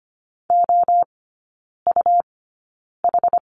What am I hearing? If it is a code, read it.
Morse code, heard as a keyed tone, about 25 wpm, spaced as for 9 wpm, 700 Hz. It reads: OUH
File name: Lesson 17e - Letter u - Three Characters 25wpm